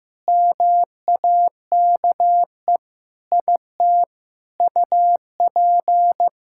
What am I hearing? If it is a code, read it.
Morse code, heard as a keyed tone, about 15 wpm, 700 Hz. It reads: MAKE IT UP
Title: Sentences from Top 1000 Words - Review Part 6 - 15wpm